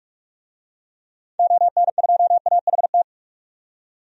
Morse code, heard as a keyed tone, about 47 wpm, 690 Hz. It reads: ON2AHT
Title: DXCC-100 Call Signs - 6 Characters - 45wpm